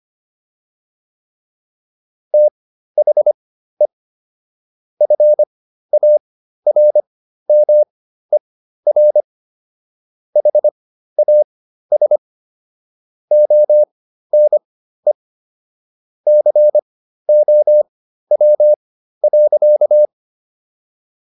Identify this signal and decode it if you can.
Morse code, heard as a keyed tone, about 25 wpm, spaced as for 13 wpm, 600 Hz. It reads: THE FARMER HAS ONE COW.